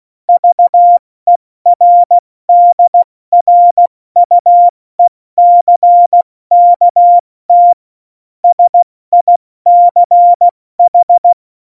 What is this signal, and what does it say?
Morse code, heard as a keyed tone, about 16 wpm, spaced as for 14 wpm, 700 Hz. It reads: VERDRUECKT SICH
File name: Morsekurs / Witze / Witz 13. November 2018, 70 F80